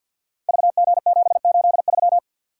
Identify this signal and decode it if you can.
Morse code, heard as a keyed tone, about 52 wpm, 700 Hz. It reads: VG783
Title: Code Groups - 5 Letter:Numbers - 50wpm